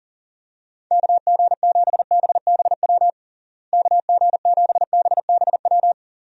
Morse code, heard as a keyed tone, about 40 wpm, 700 Hz. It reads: KG7BBW KG7BBW
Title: USA Call Signs - 6 Characters - Repeated 3x - 40wpm